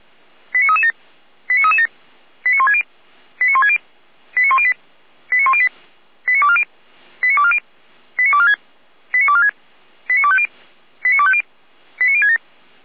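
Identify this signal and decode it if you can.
Five-tone selective call (ZVEI), heard as quick runs of five tones, tones of about 70 ms. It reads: 89337 89337 89160 89160 89108 89108 89240 89240 89246 89246 89250 89250 89067
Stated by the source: ZVEI five-tone selective call received by SDR.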